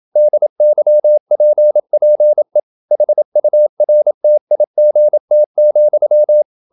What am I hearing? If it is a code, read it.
Morse code, heard as a keyed tone, about 27 wpm, 600 Hz. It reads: DYPPE HURTIGT,
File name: Ferie QRQ 27wpm